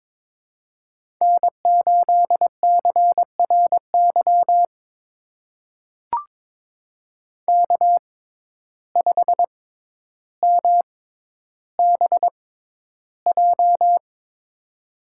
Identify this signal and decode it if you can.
Morse code, heard as a keyed tone, about 22 wpm, spaced as for 11 wpm, 700 Hz. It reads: N8CRY K5MBJ
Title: USA Call Signs - 5 Characters - 6x Character Spacing - 22wpm